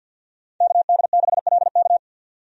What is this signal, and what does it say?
Morse code, heard as a keyed tone, about 51 wpm, 700 Hz. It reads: KD6LK
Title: USA Call Signs - 5 Characters - 50wpm